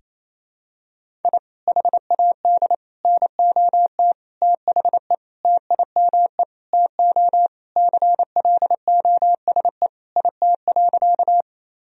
Morse code, heard as a keyed tone, about 28 wpm, 700 Hz. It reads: I HAD NOT THE TIME TO CLOSE IT.
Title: Sentences from Top 1000 Words - Review Part 1 - 28wpm